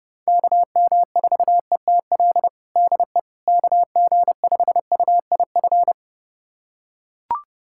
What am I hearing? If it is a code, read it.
Morse code, heard as a keyed tone, about 30 wpm, 700 Hz. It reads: KM4ETL DE KG5UIF
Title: USA Callsign DE Callsign - 6 Characters - 30wpm